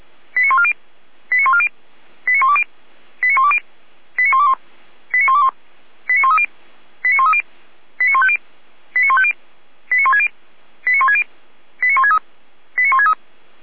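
Five-tone selective call (ZVEI), heard as quick runs of five tones, tones of about 70 ms. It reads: 89140 89140 89120 89120 89121 89121 89130 89130 89150 89150 89160 89160 89163 89163
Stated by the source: ZVEI five-tone call received via SDR.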